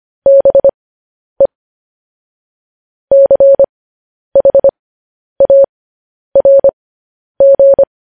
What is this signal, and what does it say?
Morse code, heard as a keyed tone, about 25 wpm, spaced as for 10 wpm, 550 Hz. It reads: BE CHARG
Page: News Headlines in Morse Code at 10 WPM